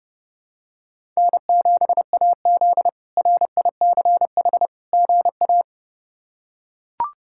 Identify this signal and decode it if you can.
Morse code, heard as a keyed tone, about 30 wpm, 700 Hz. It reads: N7AZ RICH GA
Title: K1USN SST QSO Practice - 30wpm